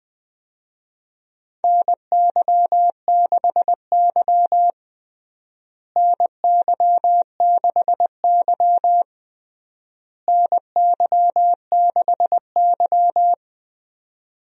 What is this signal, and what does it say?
Morse code, heard as a keyed tone, about 20 wpm, 700 Hz. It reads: NY6Y NY6Y NY6Y
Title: USA Call Signs - 4 Characters - Repeated 3x - 20wpm